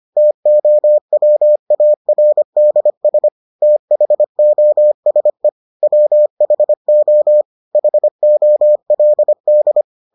Morse code, heard as a keyed tone, about 25 wpm, 600 Hz. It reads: TOWARDS THOSE WHO HOLD